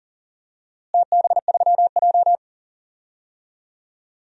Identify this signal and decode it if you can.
Morse code, heard as a keyed tone, about 40 wpm, 700 Hz. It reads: TB3J
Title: DXCC-100 Call Signs - 4 Characters - Repeated 2x - 40wpm